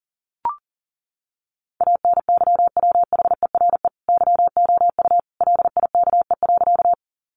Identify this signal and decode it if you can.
Morse code, heard as a keyed tone, about 40 wpm, 700 Hz. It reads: ANYWHERE YOU LIKE.
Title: Sentences from Top 1400 Words 40wpm